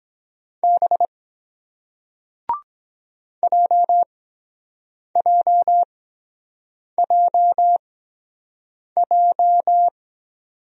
Morse code, heard as a keyed tone, about 23 wpm, spaced as for 7 wpm, 700 Hz. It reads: B JJJJ